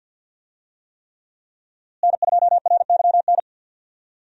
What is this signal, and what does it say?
Morse code, heard as a keyed tone, about 48 wpm, 700 Hz. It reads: N2RYN
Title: USA Call Signs - 5 Characters - Repeated 3x - 50wpm